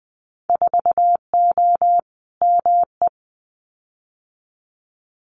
Morse code, heard as a keyed tone, about 20 wpm, 700 Hz. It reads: 4O ME
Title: Winter Field Day Exchanges Encore 20wpm